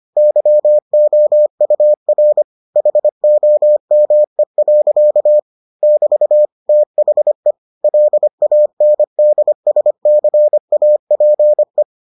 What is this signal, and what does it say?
Morse code, heard as a keyed tone, about 25 wpm, 600 Hz. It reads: YOUR HOME. <BT> THE LANDSCAPE